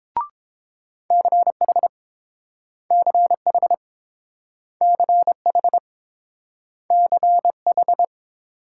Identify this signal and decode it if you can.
Morse code, heard as a keyed tone, about 30 wpm, 700 Hz. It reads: CH CH CH CH